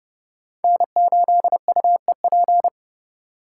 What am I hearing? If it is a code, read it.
Morse code, heard as a keyed tone, about 28 wpm, 700 Hz. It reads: N8UEP
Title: Code Groups - 5 Letter:Numbers - 30wpm